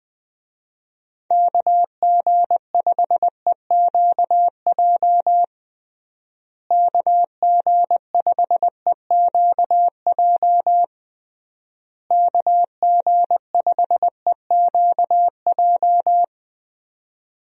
Morse code, heard as a keyed tone, about 20 wpm, 700 Hz. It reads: KG5EQJ KG5EQJ KG5EQJ
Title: USA Call Signs - 6 Characters - Repeated 3x - 20wpm